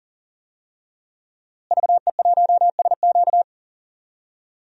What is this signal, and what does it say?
Morse code, heard as a keyed tone, about 40 wpm, 700 Hz. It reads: VE1SQ
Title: DXCC-100 Call Signs - 5 Characters - Repeated 2x - 40wpm